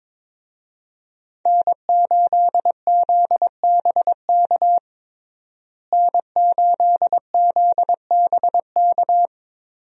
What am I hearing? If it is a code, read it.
Morse code, heard as a keyed tone, about 22 wpm, 700 Hz. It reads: N8ZBK N8ZBK
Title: USA Call Signs - 5 Characters - Repeated 3x - 22wpm